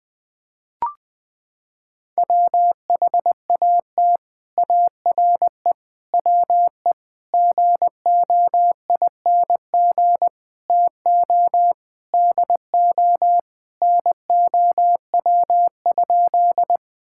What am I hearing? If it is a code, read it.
Morse code, heard as a keyed tone, about 20 wpm, 700 Hz. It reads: WHAT ARE WE GOING TO DO NOW?
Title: Sentences with 6-Word Phrases - 20wpm